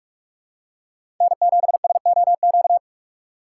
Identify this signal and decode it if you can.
Morse code, heard as a keyed tone, about 46 wpm, 700 Hz. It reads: N7SOQ
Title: USA Call Signs - 5 Characters - 3x Character Spacing - 45wpm